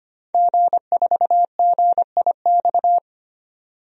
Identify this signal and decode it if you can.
Morse code, heard as a keyed tone, about 25 wpm, 700 Hz. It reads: G4GIX
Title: DXCC-100 Call Signs - 5 Characters - 25wpm